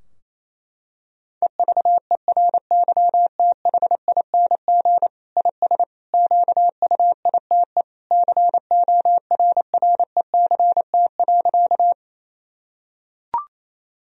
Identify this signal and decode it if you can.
Morse code, heard as a keyed tone, about 28 wpm, 700 Hz. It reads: EVERYTHING IS QUITE CORRECT.
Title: Sentences from Top 1000 Words - Review Part 2 - 28wpm